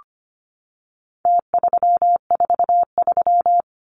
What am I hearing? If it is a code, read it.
Morse code, heard as a keyed tone, about 25 wpm, 700 Hz. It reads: T343
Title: Numbers - Contest Style - 4-Digit 25wpm